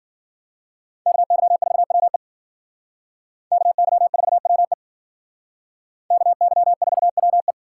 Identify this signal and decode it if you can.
Morse code, heard as a keyed tone, about 51 wpm, 700 Hz. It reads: KY4WE KY4WE KY4WE